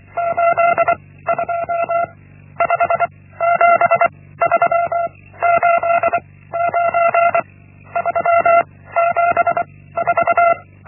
Morse code, heard as a keyed tone, about 24 wpm, 680 Hz. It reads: 8 2 5 7 3 8 9 3 7 4
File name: Linksmakalnis-1